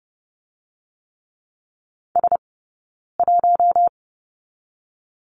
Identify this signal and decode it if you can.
Morse code, heard as a keyed tone, about 30 wpm, 700 Hz. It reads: S 1
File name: 092 Lesson 16d - Letter c - Two Characters 30wpm